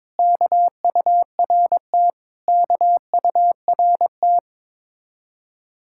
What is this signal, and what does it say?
Morse code, heard as a keyed tone, about 22 wpm, 700 Hz. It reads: KURT KURT
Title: Speed-Racing Common QSO Phrases Encore - 22wpm